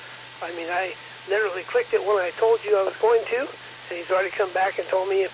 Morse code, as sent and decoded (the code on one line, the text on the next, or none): none